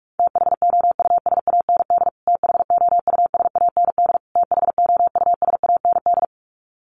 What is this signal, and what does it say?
Morse code, heard as a keyed tone, about 44 wpm, 700 Hz. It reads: THOUSAND THOUSAND THOUSAND
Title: Top 500 Words Repeated 45wpm